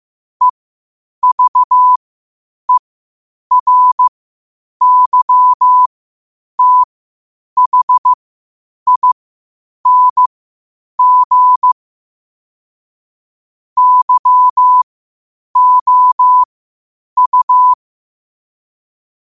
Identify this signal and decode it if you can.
Morse code, heard as a keyed tone, about 15 wpm, spaced as for 8 wpm, 1000 Hz. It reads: EVERYTHING YOU
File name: qotdfiles / 8wpm